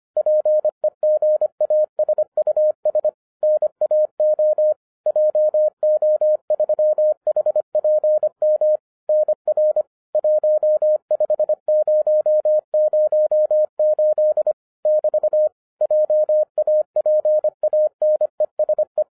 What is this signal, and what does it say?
Morse code, heard as a keyed tone, about 25 wpm, 610 Hz. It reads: PEGASUS NAO JO3HPM NR 15008 <BT> JAPANESE